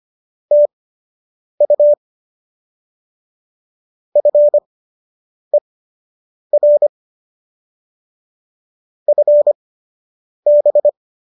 Morse code, heard as a keyed tone, about 25 wpm, spaced as for 8 wpm, 590 Hz. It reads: TU FER FB